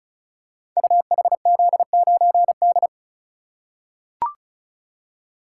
Morse code, heard as a keyed tone, about 36 wpm, 690 Hz. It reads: UHZ9D
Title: Code Groups - 5 Letter:Numbers - 35wpm